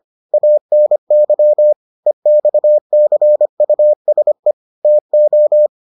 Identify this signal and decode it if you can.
Morse code, heard as a keyed tone, about 25 wpm, 600 Hz. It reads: ANY EXCUSE TO